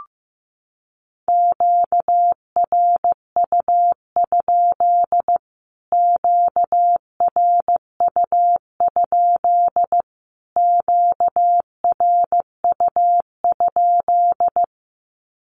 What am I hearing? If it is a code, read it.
Morse code, heard as a keyed tone, about 15 wpm, 700 Hz. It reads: QRU? QRU? QRU?
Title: Q-Codes Repeated 3x 15wpm